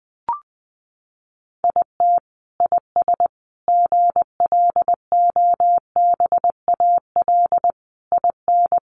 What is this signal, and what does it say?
Morse code, heard as a keyed tone, about 20 wpm, 700 Hz. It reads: IT IS GLOBAL IN
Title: Sentences from Top 1200 Words 20wpm